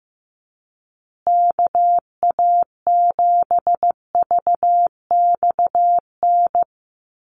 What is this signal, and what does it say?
Morse code, heard as a keyed tone, about 15 wpm, 700 Hz. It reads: KA7VXN
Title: USA Call Signs Repeated 3x 15wpm